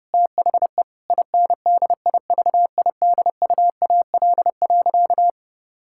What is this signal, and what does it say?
Morse code, heard as a keyed tone, about 31 wpm, 700 Hz. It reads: THE INDIVIDUAL.
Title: Sentences from Top 1000 Words - Review Part 6 - 30wpm